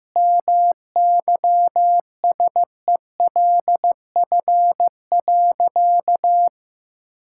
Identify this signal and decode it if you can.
Morse code, heard as a keyed tone, about 15 wpm, 700 Hz. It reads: MYSELF.
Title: Sentences from Top 1600 Words 15wpm